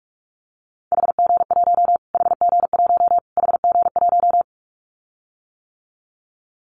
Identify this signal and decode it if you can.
Morse code, heard as a keyed tone, about 44 wpm, 700 Hz. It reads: HG1 HG1 HG1
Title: Callsign Prefixes for Top 25 DXCC - Repeated 3x - 45wpm